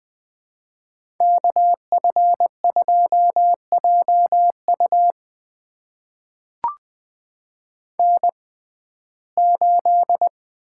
Morse code, heard as a keyed tone, about 21 wpm, spaced as for 15 wpm, 700 Hz. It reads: KF2JU N8
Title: USA Call Signs - 5 Characters - 6x Character Spacing - 20wpm